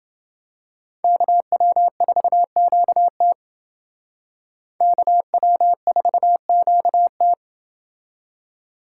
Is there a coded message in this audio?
Yes: Morse code from 1.0 to 7.3 s, heard as a keyed tone, about 28 wpm, 700 Hz. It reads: KW4QT KW4QT